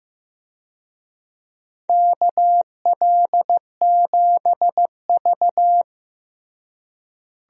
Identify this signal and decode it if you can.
Morse code, heard as a keyed tone, about 15 wpm, 700 Hz. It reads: KL7V
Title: DXCC-100 Call Signs - 4 Characters - Repeated 2x - 15wpm